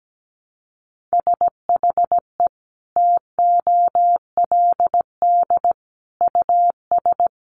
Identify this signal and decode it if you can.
Morse code, heard as a keyed tone, about 17 wpm, 700 Hz. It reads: SHE TOLD US
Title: Sentences from Top 1100 Words 17wpm